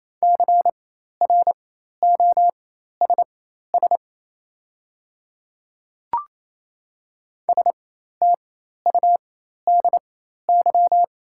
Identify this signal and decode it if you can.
Morse code, heard as a keyed tone, about 28 wpm, spaced as for 13 wpm, 700 Hz. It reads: CROSS STUDY
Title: 4x Character Spacing - Top 300 Words - 28wpm